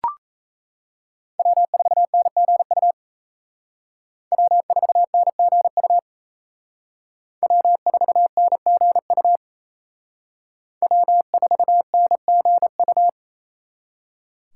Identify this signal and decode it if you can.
Morse code, heard as a keyed tone, about 36 wpm, 700 Hz. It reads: W4NGU W4NGU W4NGU W4NGU